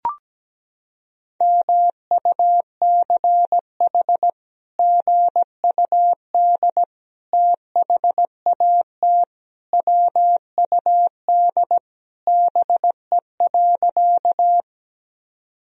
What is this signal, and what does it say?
Morse code, heard as a keyed tone, about 17 wpm, 700 Hz. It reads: MUCH GUD THAT WUD BE.
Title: Sentences from Top 200 Words and Abbreviations 17wpm